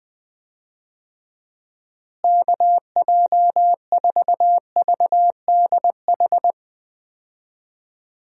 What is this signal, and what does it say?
Morse code, heard as a keyed tone, about 21 wpm, 700 Hz. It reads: KJ4VDH
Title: USA Call Signs - 6 Characters - Repeated 2x - 20wpm